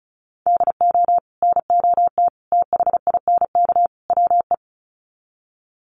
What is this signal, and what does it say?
Morse code, heard as a keyed tone, about 34 wpm, 700 Hz. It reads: DO NOT THINK WE
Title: Sets of 4 Words 35wpm